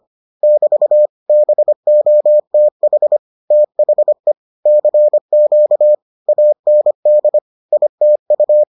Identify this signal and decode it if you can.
Morse code, heard as a keyed tone, about 25 wpm, 600 Hz. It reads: <BT> BOTH THE CQ AND ITU